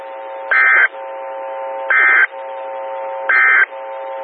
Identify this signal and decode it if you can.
Emergency Alert System (SAME) tones, heard as harsh buzzing bursts. It is heard from 0.4 to 3.7 s.
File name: WX9MCS 147255 MHz SKYWARN Repeater